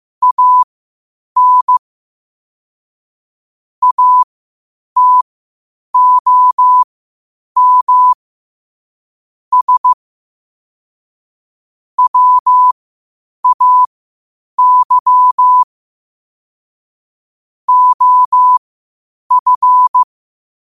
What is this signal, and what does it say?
Morse code, heard as a keyed tone, about 15 wpm, spaced as for 8 wpm, 1000 Hz. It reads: AN ATOM S WAY OF
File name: qotdfiles / 8wpm